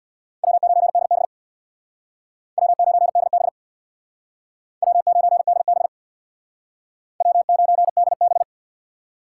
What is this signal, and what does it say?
Morse code, heard as a keyed tone, about 62 wpm, 700 Hz. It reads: W9DB W9DB W9DB W9DB
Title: Speed Racing - USA Call Signs 50wpm